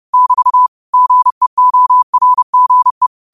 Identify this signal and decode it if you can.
Morse code, heard as a keyed tone, about 30 wpm, 1000 Hz. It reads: <BT> GEORGE
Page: Quotes for Fri, 15 Aug 2025 in Morse Code at 30 words per minute.